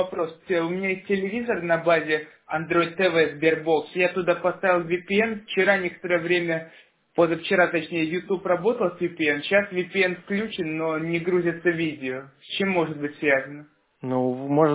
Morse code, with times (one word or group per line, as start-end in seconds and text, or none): none